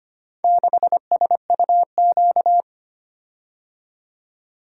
Morse code, heard as a keyed tone, about 25 wpm, 700 Hz. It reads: 6SUQ
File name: practice / 25wpm / Four Letter-Number Combinations - 25wpm